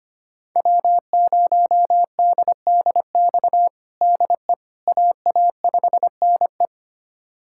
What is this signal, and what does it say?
Morse code, heard as a keyed tone, about 25 wpm, 700 Hz. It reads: W0DDX DE AA5NE
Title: USA Callsign DE Callsign - 5 Characters - 25wpm